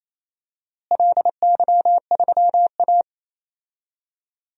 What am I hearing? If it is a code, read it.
Morse code, heard as a keyed tone, about 27 wpm, 700 Hz. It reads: LY3A
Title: DXCC-100 Call Signs - 4 Characters - Repeated 2x - 28wpm